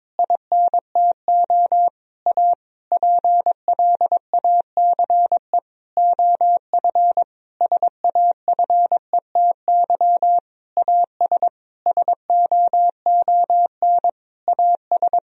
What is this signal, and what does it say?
Morse code, heard as a keyed tone, about 21 wpm, 700 Hz. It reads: INTO A PLACE OF SAFETY AS SOON AS
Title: Sentences from Top 1000 Words - Review Part 4 - 22wpm